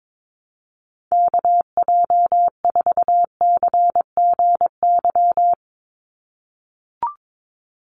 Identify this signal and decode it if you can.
Morse code, heard as a keyed tone, about 22 wpm, 700 Hz. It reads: KJ4CGY
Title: USA Call Signs 22wpm